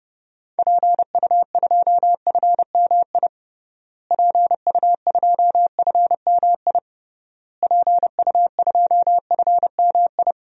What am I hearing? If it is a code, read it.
Morse code, heard as a keyed tone, about 32 wpm, 700 Hz. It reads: PU2FMI PU2FMI PU2FMI
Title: DXCC-100 Call Signs - 6 Characters - Repeated 3x - 30wpm